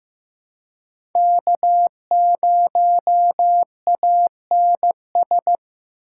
Morse code, heard as a keyed tone, about 15 wpm, 700 Hz. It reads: K0ANS
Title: USA Call Signs - 5 Characters - 2x Character Spacing - 15wpm